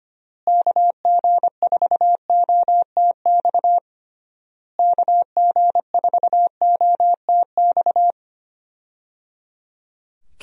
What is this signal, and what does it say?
Morse code, heard as a keyed tone, about 25 wpm, 700 Hz. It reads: KG4OTX KG4OTX
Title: USA Call Signs - 6 Characters - Repeated 3x - 25wpm